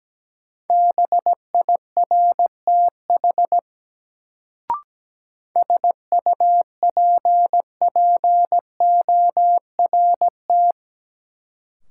Morse code, heard as a keyed tone, about 17 wpm, 700 Hz. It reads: BIRTH SUPPORT
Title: Sentences from Top 800 Words - New Word Drill - 17wpm